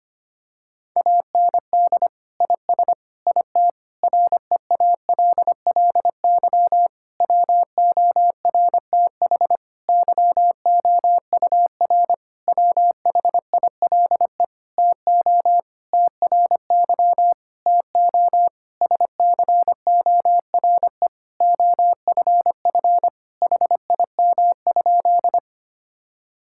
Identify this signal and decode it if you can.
Morse code, heard as a keyed tone, about 25 wpm, 700 Hz. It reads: AND IS IT REALLY WORTH YOUR WHILE TO TRY TO SCORE OFF HIM?
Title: Sentences from Top 1600 Words 25wpm